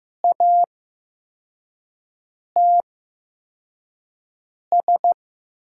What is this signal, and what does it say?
Morse code, heard as a keyed tone, about 15 wpm, spaced as for 4 wpm, 700 Hz. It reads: ATS
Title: USA Call Signs - 5 Characters - 8x Character Spacing - 15wpm